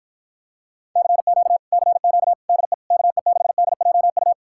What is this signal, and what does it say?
Morse code, heard as a keyed tone, about 53 wpm, 700 Hz. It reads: CQ CQ DE KE6DJU